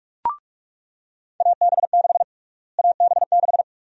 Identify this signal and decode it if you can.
Morse code, heard as a keyed tone, about 42 wpm, 700 Hz. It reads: AB6 AB6
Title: USA Call Sign Prefixes - Repeated 2x - 45wpm